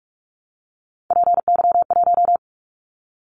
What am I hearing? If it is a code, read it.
Morse code, heard as a keyed tone, about 44 wpm, 700 Hz. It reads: PY1
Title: Callsign Prefixes for Top 100 DXCC - 45wpm